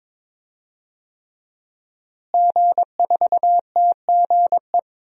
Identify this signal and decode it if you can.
Morse code, heard as a keyed tone, about 22 wpm, 700 Hz. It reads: G4TGE
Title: DXCC-100 Call Signs - 5 Characters - Repeated 3x - 22wpm